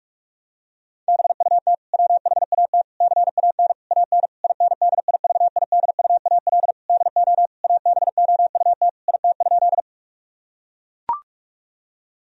Morse code, heard as a keyed tone, about 41 wpm, 700 Hz. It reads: BUT WHAT CAN AN INDIVIDUAL DO ABOUT IT?
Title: Sentences from Top 1000 Words - Review Part 4 - 45wpm